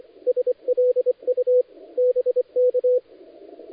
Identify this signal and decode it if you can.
Morse code, heard as a keyed tone, about 24 wpm, 500 Hz. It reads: SLU BK
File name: dlazaawansowanych / nagraniazpasma / titawaNETcontrol